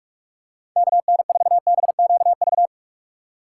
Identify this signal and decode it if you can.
Morse code, heard as a keyed tone, about 46 wpm, 700 Hz. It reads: KN4BQV
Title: USA Call Signs - 6 Characters - 2x Character Spacing - 45wpm